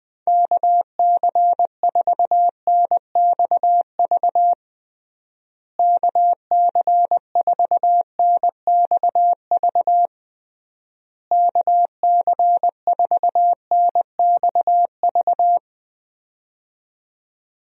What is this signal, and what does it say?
Morse code, heard as a keyed tone, about 20 wpm, 700 Hz. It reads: KC4NXV KC4NXV KC4NXV
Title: USA Call Signs - 6 Characters - Repeated 3x - 20wpm